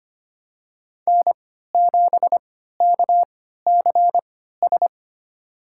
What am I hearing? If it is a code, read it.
Morse code, heard as a keyed tone, about 25 wpm, spaced as for 14 wpm, 700 Hz. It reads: N7KCS
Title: USA Call Signs - 3x Character Spacing - 25wpm